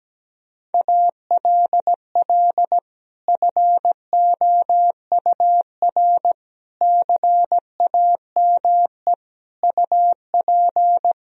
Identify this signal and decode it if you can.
Morse code, heard as a keyed tone, about 17 wpm, 700 Hz. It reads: ALL FOUR CAME UP
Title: Sentences from Top 1000 Words - Review Part 1 - 17wpm